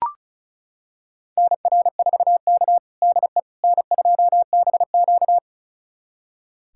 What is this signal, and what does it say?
Morse code, heard as a keyed tone, about 34 wpm, 700 Hz. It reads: NR4K DE N2BQ
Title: USA Callsign DE Callsign - 4 Characters - 35wpm